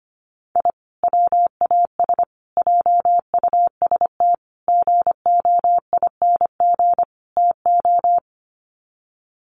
Morse code, heard as a keyed tone, about 25 wpm, 700 Hz. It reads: I WAS JUST GOING TO
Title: Sets of 5 Words Encore 25wpm